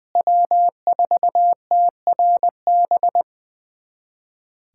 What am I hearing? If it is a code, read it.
Morse code, heard as a keyed tone, about 20 wpm, 700 Hz. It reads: W4TRB
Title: USA Callsign DE Callsign - 5 Characters - 20wpm